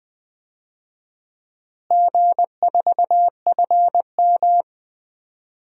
Morse code, heard as a keyed tone, about 20 wpm, 700 Hz. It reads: G4FM
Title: DXCC-100 Call Signs - 4 Characters - Repeated 2x - 20wpm